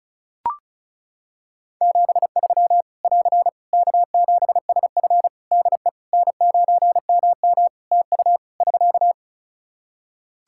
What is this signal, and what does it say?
Morse code, heard as a keyed tone, about 36 wpm, 700 Hz. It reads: 73 <AR> K7SF DE N9MM TU <SK>
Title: K1USN SST QSO Practice 35wpm